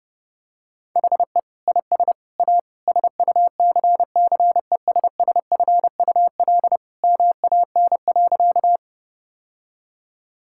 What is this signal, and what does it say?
Morse code, heard as a keyed tone, about 32 wpm, 700 Hz. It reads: HE IS A SUCCESSFUL MAN.
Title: Sentences from Top 1000 Words - Review Part 6 - 30wpm